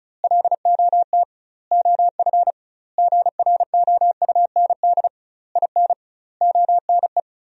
Morse code, heard as a keyed tone, about 36 wpm, 700 Hz. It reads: LOT OF GROUND IN ONE
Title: 2x Word-Spacing - Sentences from Top 500 Words - 35wpm